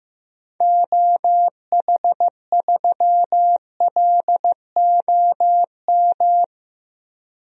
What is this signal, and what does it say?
Morse code, heard as a keyed tone, about 15 wpm, 700 Hz. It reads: OH3LOM
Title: DXCC-100 Call Signs - 6 Characters - 15wpm